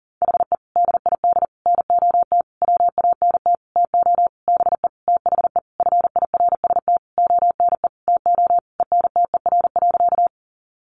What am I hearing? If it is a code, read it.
Morse code, heard as a keyed tone, about 38 wpm, 700 Hz. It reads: HE DID NOT WANT TO BE THE FIRST ONE TO ENTER.
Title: Sentences from Top 1300 Words 40wpm